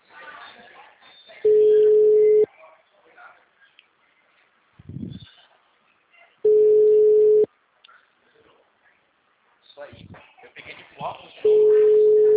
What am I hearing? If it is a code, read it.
Ringback tone (Europe, 425 Hz), with speech.